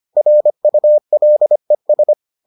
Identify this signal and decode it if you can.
Morse code, heard as a keyed tone, about 25 wpm, 600 Hz. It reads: RULES